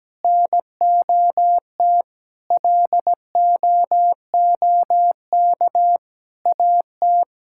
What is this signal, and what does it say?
Morse code, heard as a keyed tone, about 17 wpm, 700 Hz. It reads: NOT LOOK AT
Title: Sentences from Top 1000 Words - Review Part 6 - 17wpm